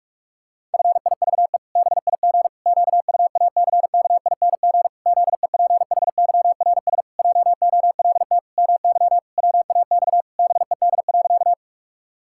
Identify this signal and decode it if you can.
Morse code, heard as a keyed tone, about 44 wpm, 700 Hz. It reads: FIVE BIG QUACKING ZEPHYRS JOLT MY WAX BED.
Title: Pangram Challenge v2 - 45wpm